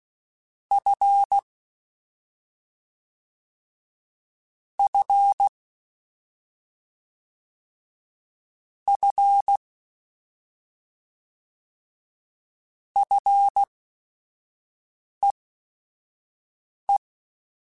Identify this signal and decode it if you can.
Morse code, heard as a keyed tone, about 16 wpm, spaced as for 5 wpm, 800 Hz. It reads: F F F FEE
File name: Short Runs / Short Runs-mp3s / SinglePrefixes @5